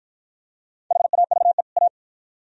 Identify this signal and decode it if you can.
Morse code, heard as a keyed tone, about 48 wpm, 700 Hz. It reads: HAVE A